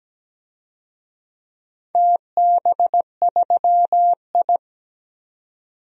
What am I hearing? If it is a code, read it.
Morse code, heard as a keyed tone, about 17 wpm, 700 Hz. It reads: TB3I
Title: DXCC-100 Call Signs - 4 Characters - 17wpm